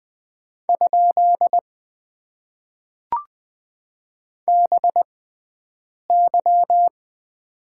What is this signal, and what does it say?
Morse code, heard as a keyed tone, about 20 wpm, spaced as for 7 wpm, 700 Hz. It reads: ? BY
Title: Top 90 QSO Elements - 6x Character Spacing - 20wpm